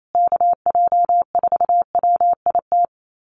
Morse code, heard as a keyed tone, about 28 wpm, 700 Hz. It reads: KJ4WIT
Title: USA Call Signs Repeated 3x 28wpm